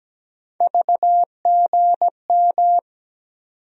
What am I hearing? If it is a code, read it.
Morse code, heard as a keyed tone, about 17 wpm, 700 Hz. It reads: VGM
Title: USA Call Sign Postfixes - 3 Letters 17wpm